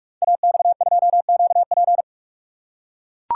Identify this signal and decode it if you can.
Morse code, heard as a keyed tone, about 44 wpm, 700 Hz. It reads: AX2QP
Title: Code Groups - 5 Letter:Numbers - 45wpm